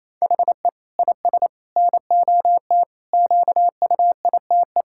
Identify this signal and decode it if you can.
Morse code, heard as a keyed tone, about 28 wpm, 700 Hz. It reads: HE IS NOT QUITE
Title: Sentences from Top 1600 Words 28wpm